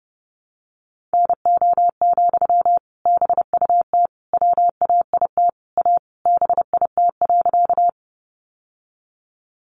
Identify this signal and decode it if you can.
Morse code, heard as a keyed tone, about 30 wpm, 700 Hz. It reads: NO, BUT WAIT A BIT.
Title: Sentences from Top 900 Words 30wpm